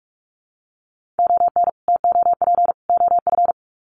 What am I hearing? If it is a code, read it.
Morse code, heard as a keyed tone, about 44 wpm, 700 Hz. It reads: ON TOP OF
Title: Sets of 3 Words 45wpm